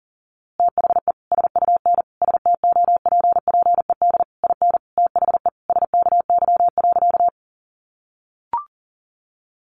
Morse code, heard as a keyed tone, about 40 wpm, 700 Hz. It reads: THE SUN STOPPED IN THE SKY.
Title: A Sound Of Thunder 40wpm